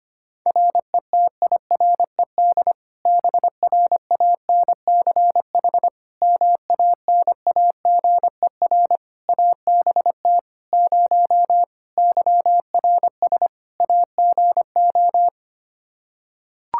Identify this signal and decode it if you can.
Morse code, heard as a keyed tone, about 25 wpm, 700 Hz. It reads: RETIRED BRANCH MANAGER ABT 0 YRS AGO
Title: Practice QSO - Exchange 3 - More Info - 25wpm